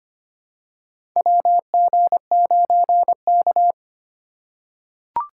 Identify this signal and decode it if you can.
Morse code, heard as a keyed tone, about 25 wpm, 700 Hz. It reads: WG9K
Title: USA Call Signs - 4 Characters - 3x Character Spacing - 25wpm